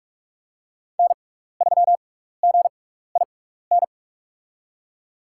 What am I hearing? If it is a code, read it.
Morse code, heard as a keyed tone, about 46 wpm, spaced as for 16 wpm, 700 Hz. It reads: N3GIN
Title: USA Call Signs - 5 Characters - 6x Character Spacing - 45wpm